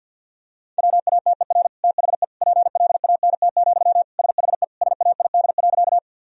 Morse code, heard as a keyed tone, about 53 wpm, 700 Hz. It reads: WATER THE PLANT, SHE SAID.
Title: Sentences from Top 1000 Words - Review Part 1 - 50wpm